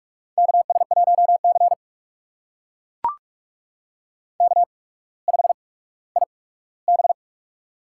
Morse code, heard as a keyed tone, about 42 wpm, spaced as for 19 wpm, 700 Hz. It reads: KS1C K5IB
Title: USA Call Signs - 4 Characters - 8x Character Spacing - 45wpm